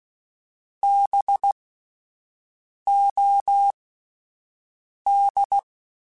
Morse code, heard as a keyed tone, about 16 wpm, spaced as for 5 wpm, 800 Hz. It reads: BOD